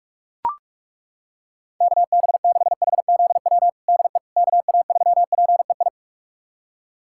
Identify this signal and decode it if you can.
Morse code, heard as a keyed tone, about 42 wpm, 700 Hz. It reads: KB6HZW DE KA3PEI
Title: USA Callsign DE Callsign - 6 Characters - 45wpm